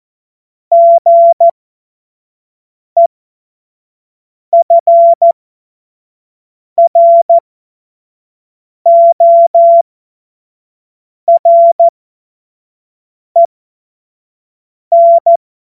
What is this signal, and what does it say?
Morse code, heard as a keyed tone, about 14 wpm, spaced as for 5 wpm, 680 Hz. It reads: GEFROREN